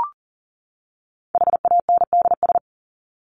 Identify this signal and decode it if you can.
Morse code, heard as a keyed tone, about 40 wpm, 700 Hz. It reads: HANDS
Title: ES:S Suffix - Top 100 Words with Base Words 40wpm